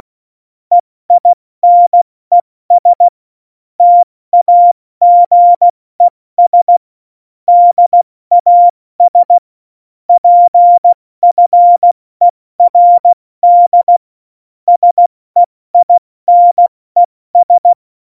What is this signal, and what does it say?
Morse code, heard as a keyed tone, about 16 wpm, 700 Hz. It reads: EINES TAGES DAS PFERD SEINES
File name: Morsekurs / Witze / Witz 21 Juli 2020, 70 F80